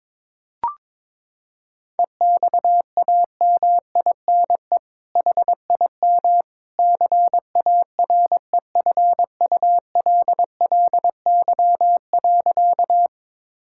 Morse code, heard as a keyed tone, about 22 wpm, 700 Hz. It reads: EXAMINE HIM CAREFULLY.
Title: Sentences from Top 1600 Words 22wpm